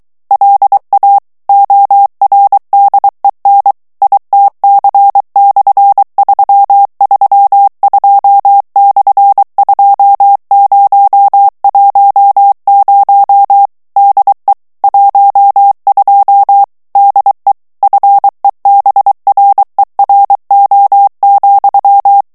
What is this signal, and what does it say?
Morse code, heard as a keyed tone, about 23 wpm, 800 Hz. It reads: LA ORDEN ITC/332/2010 DE 12 DE FEBRERO,
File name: morse mp3 / ESPAGNOL TEXTE EN CLAIR 2 VITESSE 22 MOTS MINUTES